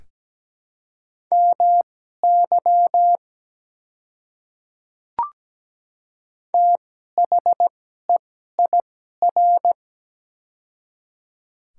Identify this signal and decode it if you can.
Morse code, heard as a keyed tone, about 17 wpm, spaced as for 12 wpm, 690 Hz. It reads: MY THEIR
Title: 2x Character Spacing - Top 100 Words - 17wpm